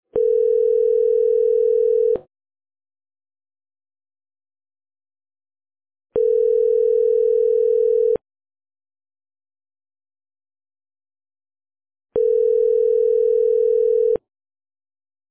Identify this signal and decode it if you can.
Ringback tone (North America), 440 + 480 Hz.